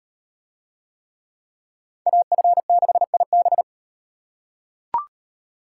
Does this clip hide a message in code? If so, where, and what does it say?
Yes: Morse code from 2.1 to 3.6 s, heard as a keyed tone, about 38 wpm, 700 Hz. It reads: AF6IB